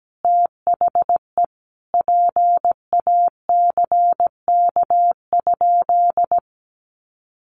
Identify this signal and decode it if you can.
Morse code, heard as a keyed tone, about 17 wpm, 700 Hz. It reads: THE PACK?
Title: Sentences from Top 1300 Words 17wpm